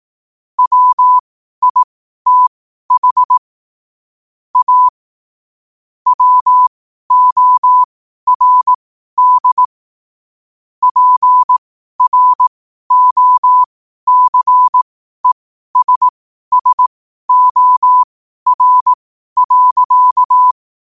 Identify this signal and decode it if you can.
Morse code, heard as a keyed tone, about 18 wpm, spaced as for 12 wpm, 1000 Hz. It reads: WITH A WORD PROCESSOR.